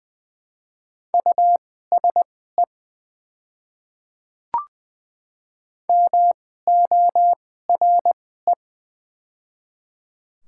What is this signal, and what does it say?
Morse code, heard as a keyed tone, about 20 wpm, spaced as for 15 wpm, 700 Hz. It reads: USE MORE
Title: 2x Character Spacing - Top 500 Words - 20wpm